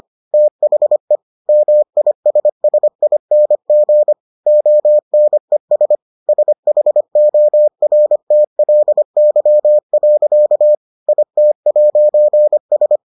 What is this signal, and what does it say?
Morse code, heard as a keyed tone, about 25 wpm, 600 Hz. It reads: THE MISSING ONES SHORTLY. IT'S